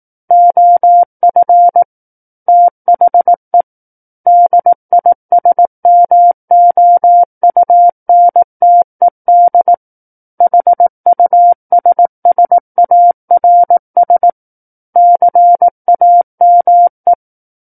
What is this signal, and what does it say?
Morse code, heard as a keyed tone, about 18 wpm, 700 Hz. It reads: OF THE DISMOUNTED HUSSARS CAME
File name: Book One / 09-Chapter 9 - 18 WPM